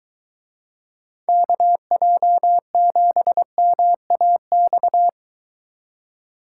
Morse code, heard as a keyed tone, about 22 wpm, 700 Hz. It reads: KJ7MAX